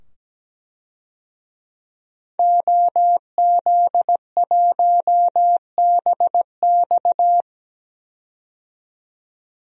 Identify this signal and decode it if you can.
Morse code, heard as a keyed tone, about 17 wpm, 700 Hz. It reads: OZ1BX
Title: DXCC-100 Call Signs - 5 Characters - 17wpm